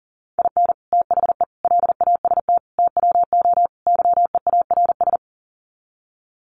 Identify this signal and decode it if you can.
Morse code, heard as a keyed tone, about 40 wpm, 700 Hz. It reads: IN THE LAST TWO YEARS
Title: Sets of 5 Words 40wpm